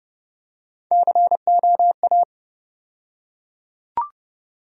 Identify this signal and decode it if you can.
Morse code, heard as a keyed tone, about 30 wpm, 700 Hz. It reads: COA
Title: USA Call Sign Postfixes - 3 Letters 30wpm